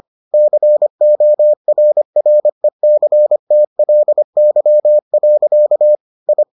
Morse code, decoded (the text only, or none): CORRECTLY. I